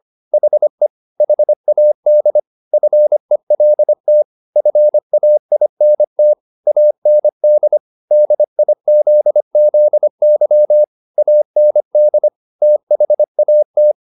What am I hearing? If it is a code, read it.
Morse code, heard as a keyed tone, about 25 wpm, 600 Hz. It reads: HE HAD FELT FAINT AND DIZZY AND THAT